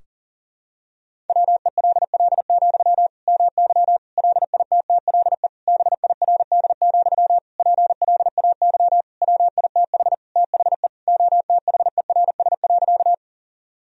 Morse code, heard as a keyed tone, about 37 wpm, 700 Hz. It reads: WELL, MY LITTLE BIRD, PLAY WITH THE OTHERS.
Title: Sentences from Top 1000 Words - Review Part 2 - 40wpm